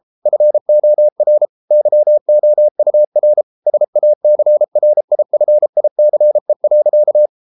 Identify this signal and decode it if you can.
Morse code, heard as a keyed tone, about 33 wpm, 600 Hz. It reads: FOR YOUR SACRIFICE.